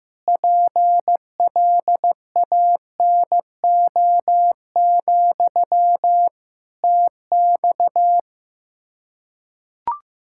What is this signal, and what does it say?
Morse code, heard as a keyed tone, about 15 wpm, 700 Hz. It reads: PLANO, TX
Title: K1USN SST QSO Practice 15wpm